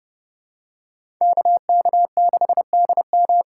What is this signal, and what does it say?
Morse code, heard as a keyed tone, about 30 wpm, 700 Hz. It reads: KK6DM
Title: USA Call Signs - 5 Characters - 3x Character Spacing - 30wpm